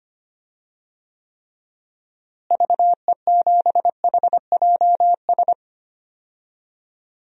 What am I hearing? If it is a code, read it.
Morse code, heard as a keyed tone, about 25 wpm, 700 Hz. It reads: VE7HJS
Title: DXCC-100 Call Signs - 6 Characters - Repeated 2x - 25wpm